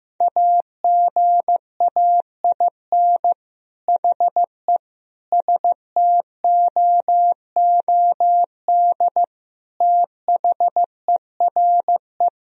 Morse code, heard as a keyed tone, about 15 wpm, 700 Hz. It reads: AGAIN HE STOOD THERE
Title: Sentences from Top 1000 Words - Review Part 3 - 15wpm